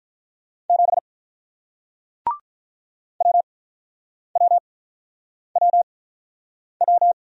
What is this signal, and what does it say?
Morse code, heard as a keyed tone, about 48 wpm, spaced as for 9 wpm, 700 Hz. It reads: 7 WWWW